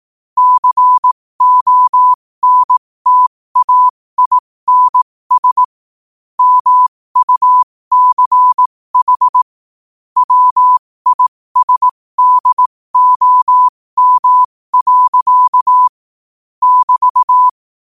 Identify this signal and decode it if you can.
Morse code, heard as a keyed tone, about 18 wpm, spaced as for 15 wpm, 1000 Hz. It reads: CONTAINS MUCH WISDOM. <BT>
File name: qotdfiles / 15wpm